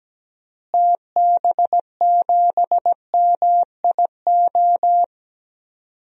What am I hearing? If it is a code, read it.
Morse code, heard as a keyed tone, about 17 wpm, 700 Hz. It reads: TB7MIO
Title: DXCC-100 Call Signs - 6 Characters - Repeated 2x - 17wpm